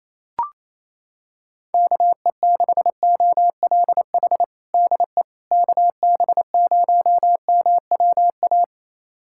Morse code, heard as a keyed tone, about 28 wpm, 700 Hz. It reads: KE6OLH DE KB0MWA
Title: USA Callsign DE Callsign - 6 Characters - 28wpm